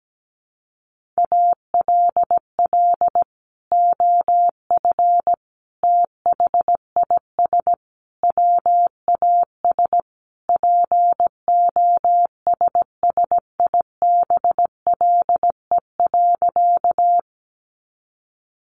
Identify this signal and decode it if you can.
Morse code, heard as a keyed tone, about 17 wpm, 700 Hz. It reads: ALL OF THIS WAS POSSIBLE.
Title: Sentences from Top 500 Words 17wpm